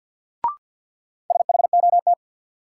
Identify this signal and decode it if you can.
Morse code, heard as a keyed tone, about 53 wpm, 700 Hz. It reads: SHOT
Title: Sentences from Top 1100 Words - New Word Drill - 50wpm